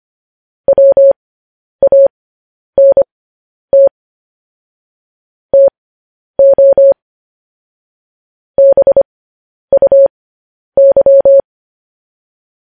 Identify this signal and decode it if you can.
Morse code, heard as a keyed tone, about 25 wpm, spaced as for 10 wpm, 550 Hz. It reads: WANT TO BUY
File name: podcast / News headlines-1x-25x10